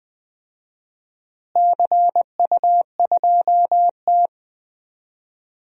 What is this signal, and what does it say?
Morse code, heard as a keyed tone, about 20 wpm, 700 Hz. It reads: CU2T
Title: DXCC-100 Call Signs - 4 Characters - Repeated 3x - 20wpm